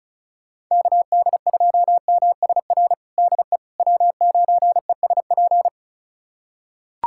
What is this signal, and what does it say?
Morse code, heard as a keyed tone, about 36 wpm, 700 Hz. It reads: KD2MSR DE W9ESP
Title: USA Callsign DE Callsign 35wpm